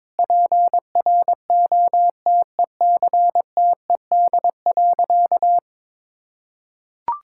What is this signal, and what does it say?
Morse code, heard as a keyed tone, about 21 wpm, 700 Hz. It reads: PROTECTED.
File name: Sentences with 9-Word Phrases - 22wpm